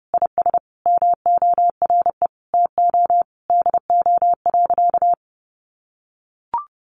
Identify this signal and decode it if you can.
Morse code, heard as a keyed tone, about 30 wpm, 700 Hz. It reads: IS MORE TO DO.
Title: Sentences from Top 100 Words 30wpm